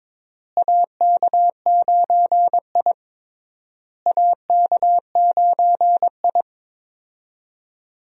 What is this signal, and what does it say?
Morse code, heard as a keyed tone, about 22 wpm, 700 Hz. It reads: AK9I AK9I
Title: USA Call Signs - 4 Characters - Repeated 3x - 22wpm